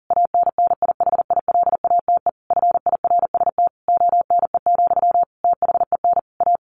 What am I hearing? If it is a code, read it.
Morse code, heard as a keyed tone, about 40 wpm, 700 Hz. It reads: ANNIHILATE FIRST ONE, THEN A
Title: A Sound Of Thunder 40wpm